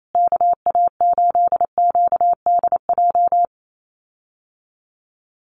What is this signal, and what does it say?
Morse code, heard as a keyed tone, about 28 wpm, 700 Hz. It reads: KA8QDJ
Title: USA Call Signs Repeated 3x 28wpm